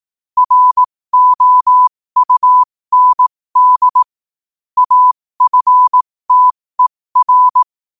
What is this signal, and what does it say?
Morse code, heard as a keyed tone, about 18 wpm, spaced as for 15 wpm, 1000 Hz. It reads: ROUND AFTER
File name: qotdfiles / 15wpm